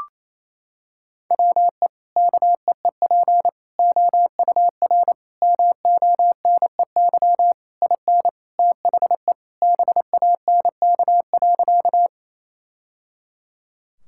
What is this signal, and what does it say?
Morse code, heard as a keyed tone, about 28 wpm, 700 Hz. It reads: WE KEEP OUR MONEY IN THE BANK.
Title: Sentences from Top 1000 Words - Review Part 6 - 28wpm